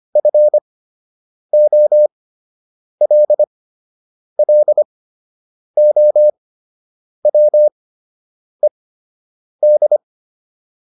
Morse code, heard as a keyed tone, about 25 wpm, spaced as for 8 wpm, 600 Hz. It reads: FOLLOWED